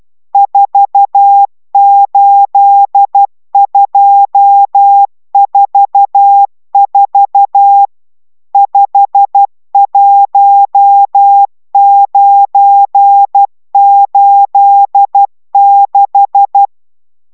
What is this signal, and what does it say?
Morse code, heard as a keyed tone, about 12 wpm, 800 Hz. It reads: 48244 51986